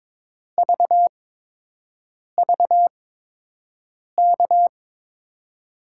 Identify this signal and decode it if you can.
Morse code, heard as a keyed tone, about 22 wpm, spaced as for 6 wpm, 700 Hz. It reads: VVK
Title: USA Call Signs - 5 Characters - 8x Character Spacing - 22wpm